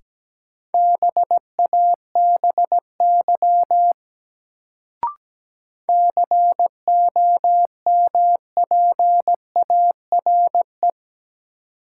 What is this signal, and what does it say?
Morse code, heard as a keyed tone, about 17 wpm, 700 Hz. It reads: BABY COMPARE
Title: Sentences from Top 1300 Words - New Word Drill Alt - 17wpm